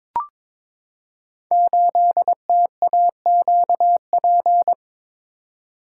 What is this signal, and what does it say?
Morse code, heard as a keyed tone, about 22 wpm, 700 Hz. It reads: 8TAQP
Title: Code Groups - 5 Letter:Numbers - 22wpm